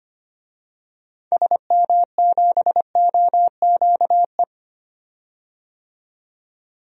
Morse code, heard as a keyed tone, about 25 wpm, 700 Hz. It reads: SM7OQE
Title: DXCC-100 Call Signs - 6 Characters - 25wpm